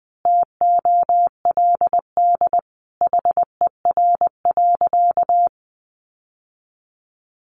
Morse code, heard as a keyed tone, about 20 wpm, 700 Hz. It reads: TOLD HER.
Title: Sentences from Top 1200 Words 20wpm